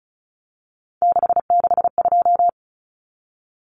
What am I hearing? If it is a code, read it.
Morse code, heard as a keyed tone, about 34 wpm, 700 Hz. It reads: 662
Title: Common Cut Numbers - 3-Digit 35wpm